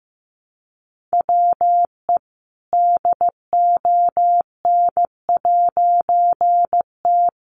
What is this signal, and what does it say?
Morse code, heard as a keyed tone, about 15 wpm, 700 Hz. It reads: WE DON'T
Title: A Sound Of Thunder 15wpm